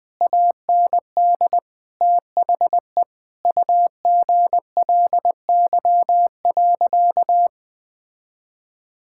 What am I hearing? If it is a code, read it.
Morse code, heard as a keyed tone, about 20 wpm, 700 Hz. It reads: AND THE UGLY.
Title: Sentences with 6-Word Phrases - 20wpm